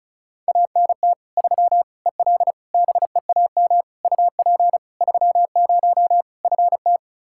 Morse code, heard as a keyed tone, about 34 wpm, 700 Hz. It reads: ANT 3 EL BEAM UP 30 FT
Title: Practice QSO - Exchange 2 - Info - 35wpm